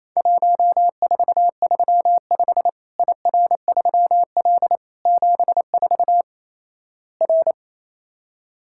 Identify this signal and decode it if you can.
Morse code, heard as a keyed tone, about 28 wpm, 700 Hz. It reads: 1435 IR3L 74 R
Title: Worked All Europe DX Contest - QTC Exchange - 28wpm